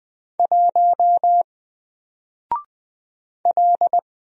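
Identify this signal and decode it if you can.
Morse code, heard as a keyed tone, about 20 wpm, 700 Hz. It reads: 1 L
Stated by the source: Single letter and number Morse drill at 20 wpm, from N4LCD.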